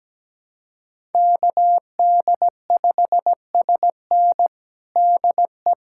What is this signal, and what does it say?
Morse code, heard as a keyed tone, about 17 wpm, 700 Hz. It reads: KD5SN DE
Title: USA Callsign DE Callsign 17wpm